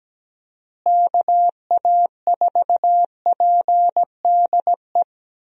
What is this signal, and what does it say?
Morse code, heard as a keyed tone, about 17 wpm, 700 Hz. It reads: KA4PDE
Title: USA Call Signs - 6 Characters - Repeated 2x - 17wpm